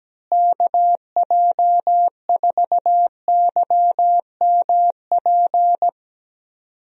Morse code, heard as a keyed tone, about 17 wpm, 700 Hz. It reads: KJ4YMP
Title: USA Call Signs - 6 Characters - Repeated 3x - 17wpm